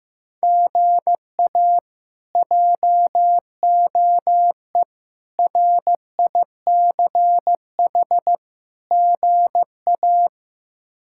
Morse code, heard as a keyed tone, about 16 wpm, 700 Hz. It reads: GA JOE RICH GA
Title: K1USN SST QSO Practice - 15wpm